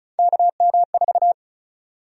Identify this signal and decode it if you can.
Morse code, heard as a keyed tone, about 36 wpm, 700 Hz. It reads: KM4
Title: USA Call Sign Prefixes - 35wpm